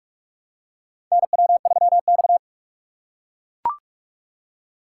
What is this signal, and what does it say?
Morse code, heard as a keyed tone, about 42 wpm, 700 Hz. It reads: NW3X
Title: USA Call Signs - 4 Characters - 3x Character Spacing - 45wpm